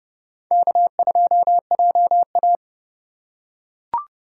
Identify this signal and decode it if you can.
Morse code, heard as a keyed tone, about 29 wpm, 700 Hz. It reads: K2JA